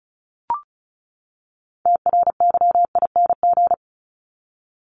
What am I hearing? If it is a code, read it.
Morse code, heard as a keyed tone, about 34 wpm, 700 Hz. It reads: TRYING
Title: ING Suffix - Top 100 Words with Base Words 35wpm